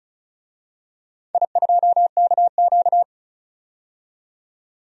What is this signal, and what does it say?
Morse code, heard as a keyed tone, about 36 wpm, 700 Hz. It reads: I2KQ
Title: DXCC-100 Call Signs - 4 Characters - 35wpm